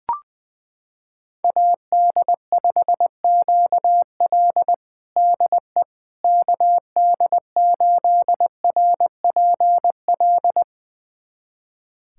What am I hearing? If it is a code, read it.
Morse code, heard as a keyed tone, about 20 wpm, 700 Hz. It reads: AD5QL DE KD8RPL
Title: USA Callsign DE Callsign 20wpm